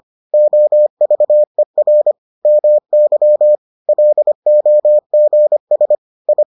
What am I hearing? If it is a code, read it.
Morse code, heard as a keyed tone, about 25 wpm, 600 Hz. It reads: OVER MY LOGS I